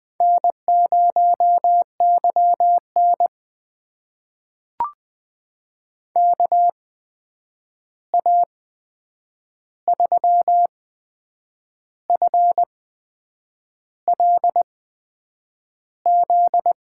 Morse code, heard as a keyed tone, about 20 wpm, spaced as for 7 wpm, 700 Hz. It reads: N0YN KA3FLZ
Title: USA Call Signs - 8x Character Spacing - 20wpm